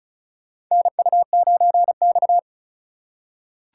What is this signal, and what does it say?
Morse code, heard as a keyed tone, about 36 wpm, 700 Hz. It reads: NU9X
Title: USA Call Signs - 4 Characters - 3x Character Spacing - 35wpm